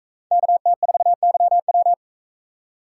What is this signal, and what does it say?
Morse code, heard as a keyed tone, about 44 wpm, 700 Hz. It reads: KT4YW